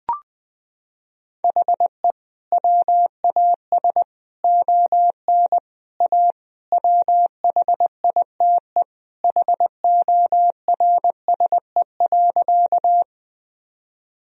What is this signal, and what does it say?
Morse code, heard as a keyed tone, about 21 wpm, 700 Hz. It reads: HE WAS ON A WHITE HORSE.
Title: Sentences from Top 1000 Words - Review Part 1 - 20wpm